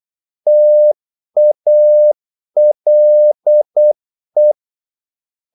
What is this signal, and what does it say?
Morse code, heard as a keyed tone, about 8 wpm, 600 Hz. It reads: TALE
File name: Korte ord DK 8wpm